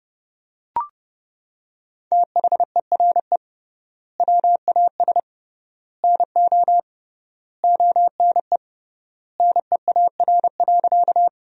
Morse code, heard as a keyed tone, about 32 wpm, 700 Hz. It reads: THERE WAS NO ONE NEAR.
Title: 3x Word-Spacing - Sentences from Top 200 Words - 30wpm